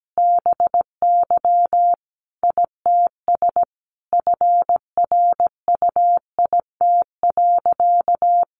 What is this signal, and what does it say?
Morse code, heard as a keyed tone, about 17 wpm, 700 Hz. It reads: BY ITS FRUIT.
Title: Sentences from Top 1300 Words 17wpm